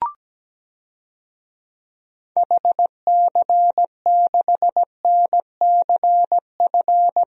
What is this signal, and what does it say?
Morse code, heard as a keyed tone, about 17 wpm, 700 Hz. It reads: HC6NCF
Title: DXCC-100 Call Signs - 6 Characters - Repeated 3x - 17wpm